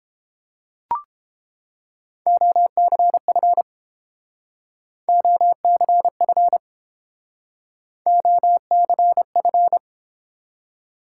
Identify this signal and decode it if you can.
Morse code, heard as a keyed tone, about 30 wpm, 700 Hz. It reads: OCF OCF OCF